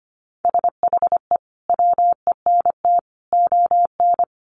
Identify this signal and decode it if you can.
Morse code, heard as a keyed tone, about 25 wpm, 700 Hz. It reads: SHE WENT ON
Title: Sentences from Top 800 Words 25wpm